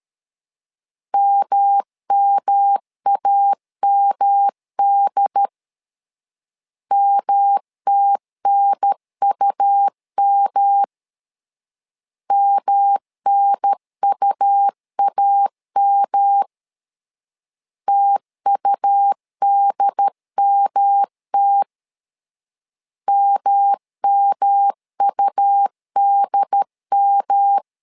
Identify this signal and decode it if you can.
Morse code, heard as a keyed tone, about 13 wpm, 800 Hz. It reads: MMAMD MTNUM MNUAM TUDMT MMUDM
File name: LEZIONI / 3 MWRO / 3 M 12wpm 1541